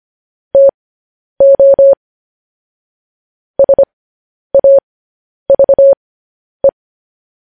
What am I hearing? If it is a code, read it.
Morse code, heard as a keyed tone, about 25 wpm, spaced as for 10 wpm, 550 Hz. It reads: TO SAVE